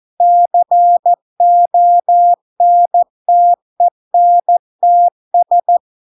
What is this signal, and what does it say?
Morse code, heard as a keyed tone, about 14 wpm, 700 Hz. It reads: CONTENTS